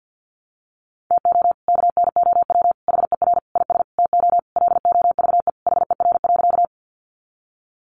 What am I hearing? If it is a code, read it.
Morse code, heard as a keyed tone, about 50 wpm, 700 Hz. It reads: TO KNOW HER IS TO LOVE HER.
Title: Sentences from Top 400 Words 50wpm